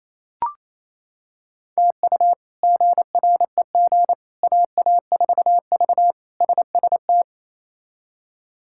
Morse code, heard as a keyed tone, about 28 wpm, 700 Hz. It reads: TU GREG AA4V SST
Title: K1USN SST QSO Practice - 28wpm